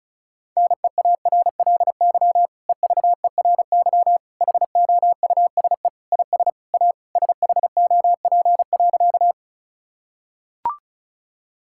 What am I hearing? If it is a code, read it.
Morse code, heard as a keyed tone, about 36 wpm, 700 Hz. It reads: NEARLY EVERY HOUSE IS A SHOP.
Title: Sentences from Top 1600 Words 35wpm